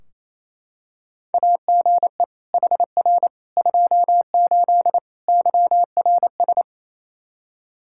Morse code, heard as a keyed tone, about 28 wpm, 700 Hz. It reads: AGE HR 28 YRS
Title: Practice QSO - Exchange 3 - More Info - 28wpm